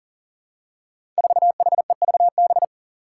Morse code, heard as a keyed tone, about 40 wpm, 700 Hz. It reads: 4HEVB
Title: Five Challenging Letter-Number Combinations 40wpm